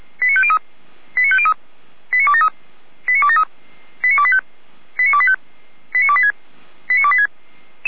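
Five-tone selective call (ZVEI), heard as quick runs of five tones, tones of about 70 ms. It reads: 89503 89503 89273 89273 89285 89285 89286 89286
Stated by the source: ZVEI five-tone selective call 89220 received with an SDR, on the feuerwehr-orlamuende fire brigade server.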